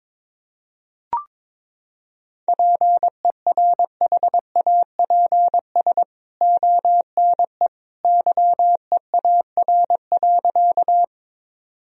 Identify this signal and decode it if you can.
Morse code, heard as a keyed tone, about 22 wpm, 700 Hz. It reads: PERHAPS ONE YEAR.
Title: Sentences from Top 1000 Words - Review Part 1 - 22wpm